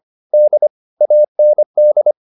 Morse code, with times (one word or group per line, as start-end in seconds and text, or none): 0.3-0.7 D
1.0-2.1 AND